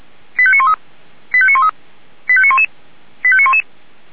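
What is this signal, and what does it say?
Five-tone selective call (ZVEI), heard as quick runs of five tones, tones of about 70 ms. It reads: 85913 85913 85911 85911
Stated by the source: ZVEI five-tone call 85727 received by SDR.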